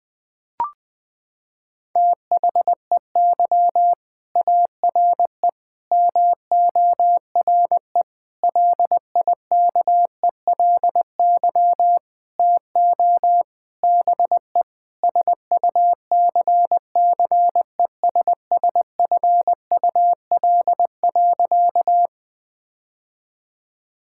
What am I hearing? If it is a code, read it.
Morse code, heard as a keyed tone, about 20 wpm, 700 Hz. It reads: THEY ARE MORE LIKELY TO BE SUCCESSFUL.
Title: Sentences with 5-Word Phrases - 20wpm